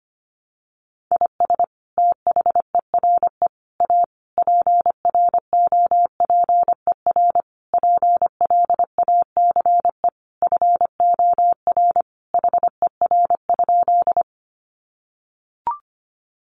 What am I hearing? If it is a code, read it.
Morse code, heard as a keyed tone, about 25 wpm, 700 Hz. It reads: IS THERE A PROPER PLACE FOR HER?
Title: Sentences from Top 800 Words 25wpm